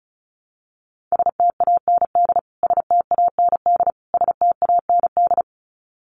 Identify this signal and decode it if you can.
Morse code, heard as a keyed tone, about 34 wpm, 700 Hz. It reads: STAND STAND STAND
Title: Top 300 Words Repeated 35wpm